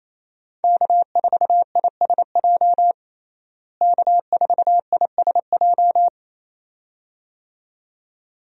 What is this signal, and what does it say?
Morse code, heard as a keyed tone, about 28 wpm, 700 Hz. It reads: K4ISJ K4ISJ
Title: USA Call Signs - 5 Characters - Repeated 2x - 28wpm